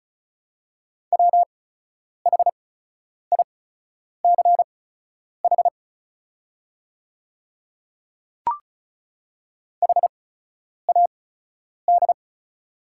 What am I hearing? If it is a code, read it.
Morse code, heard as a keyed tone, about 36 wpm, spaced as for 10 wpm, 690 Hz. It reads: WHICH HAD
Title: 8x Character Spacing - Top 100 Words - 35wpm